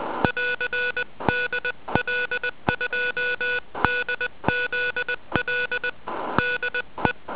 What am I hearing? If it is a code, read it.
Morse code, heard as a keyed tone, about 20 wpm, 1470 Hz. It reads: <AR> DL2DZL DE